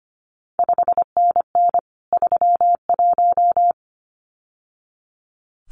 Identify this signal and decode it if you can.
Morse code, heard as a keyed tone, about 25 wpm, 700 Hz. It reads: 5NN 31
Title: CQ WW DX Contest Exchange - 25wpm